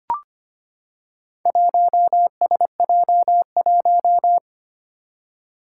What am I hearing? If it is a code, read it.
Morse code, heard as a keyed tone, about 25 wpm, 700 Hz. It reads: 1SJ1
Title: Four Challenging Letter-Number Combinations 25wpm